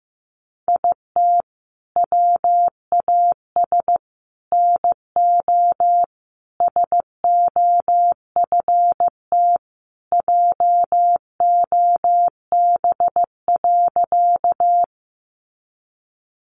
Morse code, heard as a keyed tone, about 15 wpm, 700 Hz. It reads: IT WAS NO SOFT JOB.
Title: Sentences from Top 1500 Words 15wpm